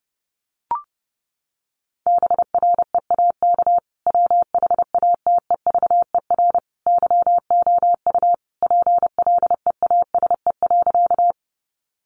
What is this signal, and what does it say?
Morse code, heard as a keyed tone, about 30 wpm, 700 Hz. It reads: BREAK WHATEVER YOU PLEASE.
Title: Sentences from Top 1500 Words 30wpm